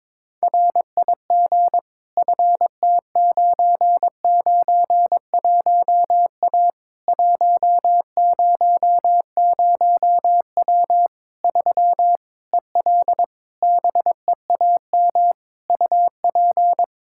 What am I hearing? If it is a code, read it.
Morse code, heard as a keyed tone, about 22 wpm, 700 Hz. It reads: RIG FT991A 100W 3 EL BEAM UP
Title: K1USN SST QSO Practice 22wpm